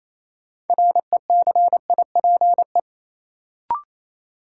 Morse code, heard as a keyed tone, about 28 wpm, 700 Hz. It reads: RECIPE
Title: Sentences from Top 1200 Words - New Word Drill - 28wpm